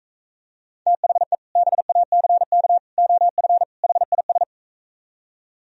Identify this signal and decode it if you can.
Morse code, heard as a keyed tone, about 45 wpm, 700 Hz. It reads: THE BACK OF HIS